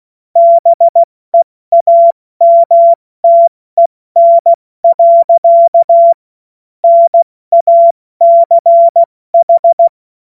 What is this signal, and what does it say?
Morse code, heard as a keyed tone, about 16 wpm, spaced as for 14 wpm, 680 Hz. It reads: BEAMTEN. NACH
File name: Witz-23-September-2025-70-F80